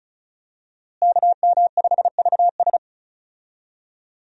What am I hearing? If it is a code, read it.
Morse code, heard as a keyed tone, about 36 wpm, 700 Hz. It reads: KM5VS
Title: USA Call Signs - 5 Characters - Repeated 3x - 35wpm